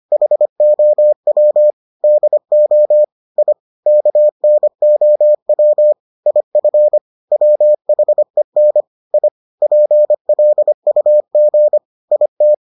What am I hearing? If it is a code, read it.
Morse code, heard as a keyed tone, about 25 wpm, 600 Hz. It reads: HOW DO I KNOW IF WHEN I PLUG IT